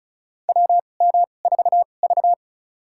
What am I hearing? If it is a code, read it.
Morse code, heard as a keyed tone, about 34 wpm, spaced as for 25 wpm, 690 Hz. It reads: WM4V
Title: USA Call Signs - 4 Characters - 2x Character Spacing - 35wpm